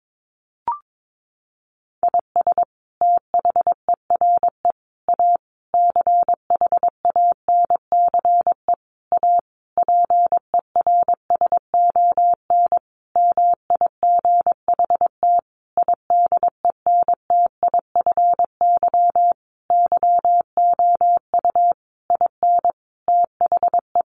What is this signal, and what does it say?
Morse code, heard as a keyed tone, about 22 wpm, 700 Hz. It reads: IS THERE A CHANCE A PERSON MIGHT IDENTIFY YOU IN THE
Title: Sentences from Top 900 Words 22wpm